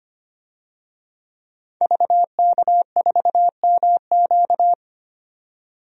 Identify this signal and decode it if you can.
Morse code, heard as a keyed tone, about 25 wpm, 700 Hz. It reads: VK4MQ
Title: DXCC-100 Call Signs - 5 Characters - 25wpm